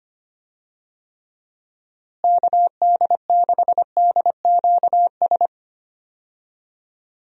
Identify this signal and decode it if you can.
Morse code, heard as a keyed tone, about 25 wpm, 700 Hz. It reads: KD6DQS
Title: USA Call Signs - 6 Characters - Repeated 2x - 25wpm